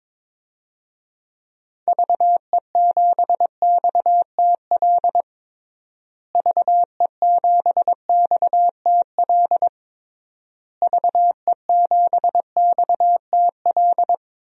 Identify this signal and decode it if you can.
Morse code, heard as a keyed tone, about 22 wpm, 700 Hz. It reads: VE7XTL VE7XTL VE7XTL
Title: DXCC-100 Call Signs - 6 Characters - Repeated 3x - 22wpm